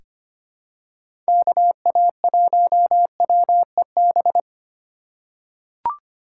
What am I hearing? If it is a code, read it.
Morse code, heard as a keyed tone, about 25 wpm, 700 Hz. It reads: KA1WEB
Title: USA Call Signs - 6 Characters - 4x Character Spacing - 25wpm